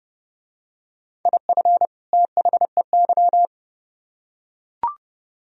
Morse code, heard as a keyed tone, about 29 wpm, 700 Hz. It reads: IF THEY